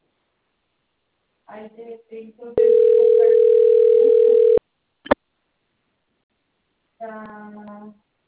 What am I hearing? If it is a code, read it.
Ringback tone (North America), 440 + 480 Hz, with speech.
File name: out- - - - - .63689